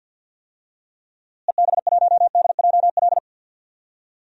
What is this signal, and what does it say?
Morse code, heard as a keyed tone, about 47 wpm, 700 Hz. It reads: EB1DJL
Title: DXCC-100 Call Signs - 6 Characters - Repeated 2x - 50wpm